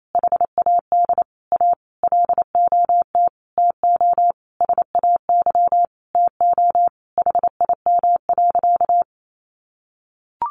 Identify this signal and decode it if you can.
Morse code, heard as a keyed tone, about 28 wpm, 700 Hz. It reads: HAD A LOT TO SAY TO HIM.
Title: Sentences from Top 1000 Words 28wpm